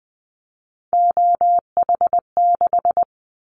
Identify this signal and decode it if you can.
Morse code, heard as a keyed tone, about 20 wpm, 700 Hz. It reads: OH6
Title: Callsign Prefixes for Top 25 DXCC - Repeated 3x - 20wpm